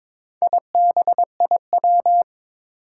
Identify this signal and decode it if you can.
Morse code, heard as a keyed tone, about 22 wpm, 700 Hz. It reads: IBIW
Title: Four Challenging Letter-Number Combinations 22wpm